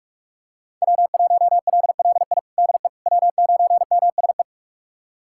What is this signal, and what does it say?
Morse code, heard as a keyed tone, about 46 wpm, 700 Hz. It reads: W1LLI DE W9MSE
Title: USA Callsign DE Callsign - 5 Characters - 45wpm